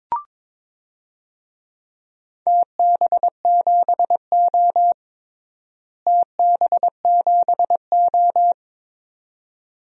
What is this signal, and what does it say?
Morse code, heard as a keyed tone, about 22 wpm, 700 Hz. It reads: TB7O TB7O
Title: DXCC-100 Call Signs - 4 Characters - Repeated 2x - 22wpm